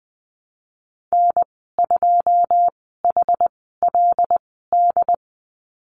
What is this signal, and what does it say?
Morse code, heard as a keyed tone, about 20 wpm, spaced as for 15 wpm, 700 Hz. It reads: N2HLD
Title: 147 Lesson 24g - Letter y - Call Signs 20wpm